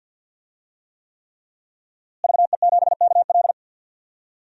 Morse code, heard as a keyed tone, about 52 wpm, 700 Hz. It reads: VE7KL
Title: DXCC-100 Call Signs - 5 Characters - Repeated 3x - 50wpm